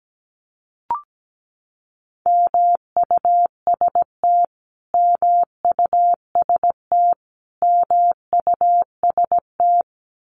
Morse code, heard as a keyed tone, about 17 wpm, 700 Hz. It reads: MUST MUST MUST
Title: Top 500 Words Repeated 17wpm